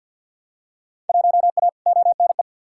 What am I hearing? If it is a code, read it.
Morse code, heard as a keyed tone, about 53 wpm, 700 Hz. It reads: 1A ONE
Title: Field Day Exchanges Encore - 50wpm